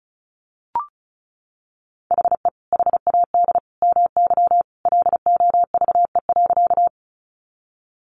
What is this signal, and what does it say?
Morse code, heard as a keyed tone, about 34 wpm, 700 Hz. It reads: HE HAD MY LOVE.
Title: Sentences from Top 400 Words 35wpm